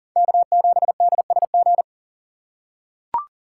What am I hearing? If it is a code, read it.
Morse code, heard as a keyed tone, about 40 wpm, 700 Hz. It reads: K7DSG
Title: USA Call Signs - 5 Characters - 2x Character Spacing - 40wpm